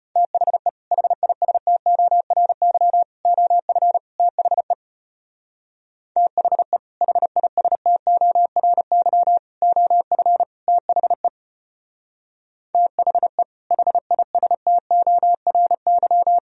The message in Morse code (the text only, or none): THE HISTORY OF THE THE HISTORY OF THE THE HISTORY